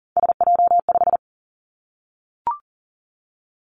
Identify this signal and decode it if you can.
Morse code, heard as a keyed tone, about 40 wpm, 700 Hz. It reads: SJ5
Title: Three Challenging Letter-Number Combinations 40wpm